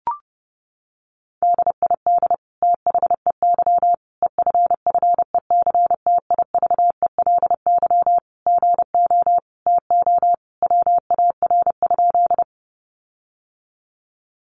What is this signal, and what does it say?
Morse code, heard as a keyed tone, about 30 wpm, 700 Hz. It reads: DID THEY EFFECTIVELY GO TO WAR?
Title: Sentences from Top 1400 Words 30wpm